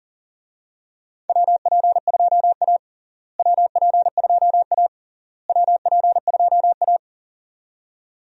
Morse code, heard as a keyed tone, about 37 wpm, 700 Hz. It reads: WP2A WP2A WP2A
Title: USA Call Signs - 4 Characters - Repeated 3x - 40wpm